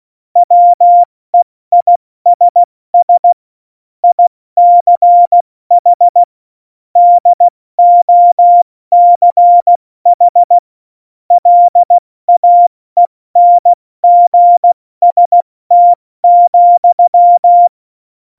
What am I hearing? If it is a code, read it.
Morse code, heard as a keyed tone, about 16 wpm, 700 Hz. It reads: WEISS ICH DOCH LAENGST,